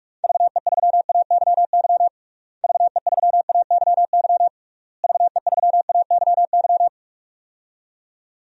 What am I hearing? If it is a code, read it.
Morse code, heard as a keyed tone, about 46 wpm, 700 Hz. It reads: VE3AYY VE3AYY VE3AYY
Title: DXCC-100 Call Signs - 6 Characters - Repeated 3x - 45wpm